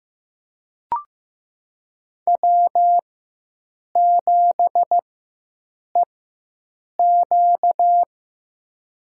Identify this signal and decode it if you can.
Morse code, heard as a keyed tone, about 15 wpm, spaced as for 7 wpm, 700 Hz. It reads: W7EQ
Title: USA Call Signs - 4 Characters - 4x Character Spacing - 15wpm